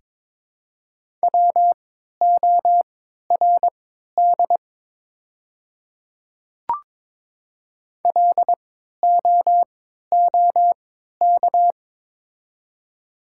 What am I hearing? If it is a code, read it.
Morse code, heard as a keyed tone, about 23 wpm, spaced as for 13 wpm, 700 Hz. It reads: WORD LOOK
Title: 3x Character Spacing - Top 100 Words - 22wpm